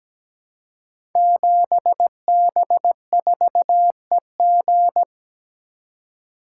Morse code, heard as a keyed tone, about 17 wpm, 700 Hz. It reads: 7B4EG
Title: Code Groups - 5 Letter:Numbers - 17wpm